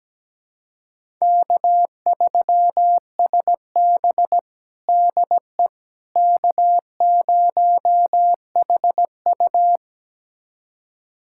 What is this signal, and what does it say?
Morse code, heard as a keyed tone, about 17 wpm, 700 Hz. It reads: K3SB DE K0HU
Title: USA Callsign DE Callsign - 4 Characters - 17wpm